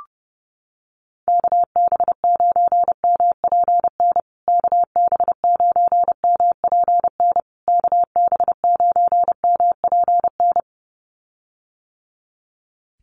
Morse code, heard as a keyed tone, about 30 wpm, 700 Hz. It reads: KB9MPN KB9MPN KB9MPN
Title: USA Call Signs Repeated 3x 30wpm